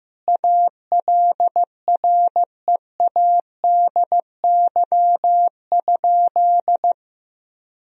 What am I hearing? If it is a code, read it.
Morse code, heard as a keyed tone, about 14 wpm, 700 Hz. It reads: ALREADY?
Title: Sentences from Top 1000 Words - Review Part 3 - 15wpm